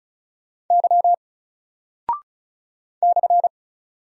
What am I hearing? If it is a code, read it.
Morse code, heard as a keyed tone, about 36 wpm, 700 Hz. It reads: Y /
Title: Single Character (Letter:Number:Punctuation:Prosign) 35wpm